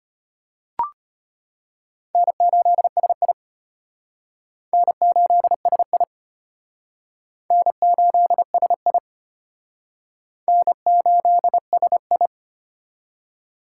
Morse code, heard as a keyed tone, about 32 wpm, 700 Hz. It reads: N8SI N8SI N8SI N8SI